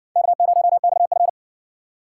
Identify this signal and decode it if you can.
Morse code, heard as a keyed tone, about 52 wpm, 700 Hz. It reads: K0XF